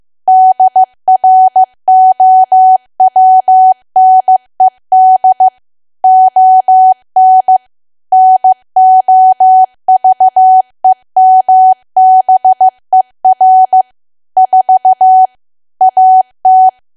Morse code, heard as a keyed tone, about 15 wpm, 750 Hz. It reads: DROWNED ON NOVEMBER 4 AT
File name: cwa / W1AW-Level2 / 15WPM